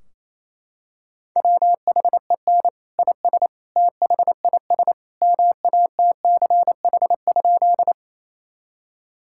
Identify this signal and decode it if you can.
Morse code, heard as a keyed tone, about 27 wpm, 700 Hz. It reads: WHEN IS THIS MATCH?
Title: Sentences from Top 1600 Words 28wpm